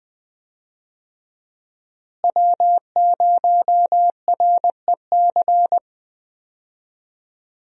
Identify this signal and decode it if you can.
Morse code, heard as a keyed tone, about 20 wpm, 700 Hz. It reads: W0REC
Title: USA Call Signs - 5 Characters - Repeated 3x - 20wpm